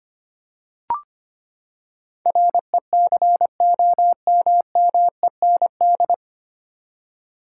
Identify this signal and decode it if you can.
Morse code, heard as a keyed tone, about 25 wpm, 700 Hz. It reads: RECOMMEND
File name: Nine Letter Words - 25wpm